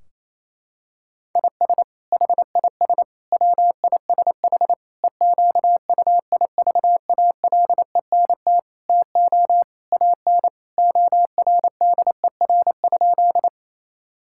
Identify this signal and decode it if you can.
Morse code, heard as a keyed tone, about 28 wpm, 700 Hz. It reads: IS HIS WISH EQUIVALENT TO AN ORDER?
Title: Sentences from Top 1600 Words 28wpm